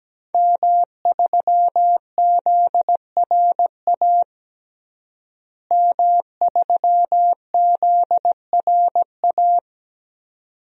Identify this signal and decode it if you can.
Morse code, heard as a keyed tone, about 17 wpm, 700 Hz. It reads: M3ZRA M3ZRA
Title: DXCC-100 Call Signs - 5 Characters - Repeated 2x - 17wpm